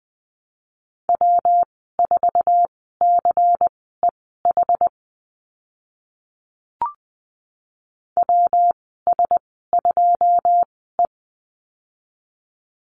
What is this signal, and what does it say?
Morse code, heard as a keyed tone, about 20 wpm, spaced as for 15 wpm, 700 Hz. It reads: W4CEH WS2E
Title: 140 Lesson 23g - Letter f - Call Signs 20wpm